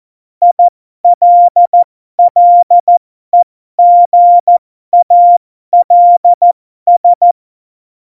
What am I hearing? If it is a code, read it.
Morse code, heard as a keyed tone, about 14 wpm, spaced as for 12 wpm, 700 Hz. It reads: ILLEGALS
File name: Morsekurs / Witze